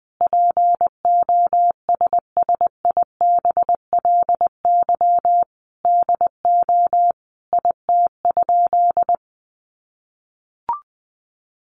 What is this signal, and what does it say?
Morse code, heard as a keyed tone, about 20 wpm, 700 Hz. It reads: POSSIBLY DO IT?
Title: Sentences from Top 1300 Words 20wpm